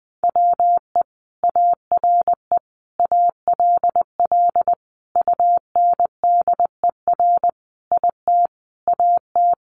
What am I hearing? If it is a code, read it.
Morse code, heard as a keyed tone, about 20 wpm, 700 Hz. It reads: WE ARE ALL UNDER IT AT
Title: Sentences from Top 1500 Words 20wpm